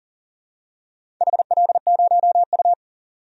Morse code, heard as a keyed tone, about 40 wpm, 700 Hz. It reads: HL0U
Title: DXCC-100 Call Signs - 4 Characters - Repeated 2x - 40wpm